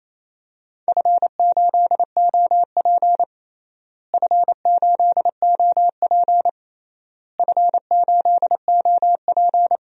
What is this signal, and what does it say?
Morse code, heard as a keyed tone, about 28 wpm, 700 Hz. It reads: F8OP F8OP F8OP
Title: DXCC-100 Call Signs - 4 Characters - Repeated 3x - 28wpm